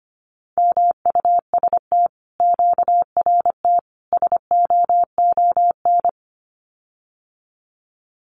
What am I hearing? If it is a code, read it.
Morse code, heard as a keyed tone, about 25 wpm, 700 Hz. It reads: MUST QRT SOON
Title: Common QSO Phrases Encore 25wpm